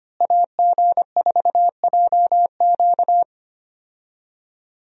Morse code, heard as a keyed tone, about 25 wpm, 700 Hz. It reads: AG4JQ
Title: USA Call Signs - 5 Characters - Repeated 2x - 25wpm